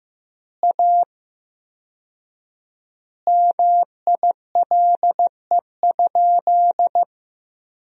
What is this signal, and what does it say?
Morse code, heard as a keyed tone, about 15 wpm, 700 Hz. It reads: A MILE?
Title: 4x Word-Spacing - Sentences from Top 400 Words - 15wpm